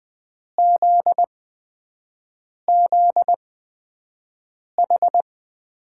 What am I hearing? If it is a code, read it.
Morse code, heard as a keyed tone, about 20 wpm, spaced as for 5 wpm, 700 Hz. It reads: ZZH
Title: USA Call Signs - 6 Characters - 8x Character Spacing - 20wpm